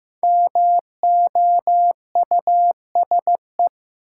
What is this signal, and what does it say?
Morse code, heard as a keyed tone, about 15 wpm, 700 Hz. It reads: MOUSE
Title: Sentences from Top 700 Words - New Word Drill - 15wpm